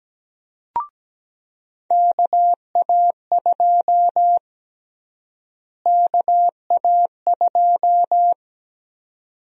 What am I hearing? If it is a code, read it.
Morse code, heard as a keyed tone, about 17 wpm, 700 Hz. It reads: KA2 KA2
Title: USA Call Sign Prefixes - Repeated 2x - 17wpm